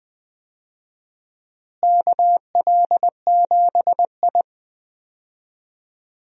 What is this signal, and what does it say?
Morse code, heard as a keyed tone, about 20 wpm, 700 Hz. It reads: KL7I
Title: DXCC-100 Call Signs - 4 Characters - Repeated 2x - 20wpm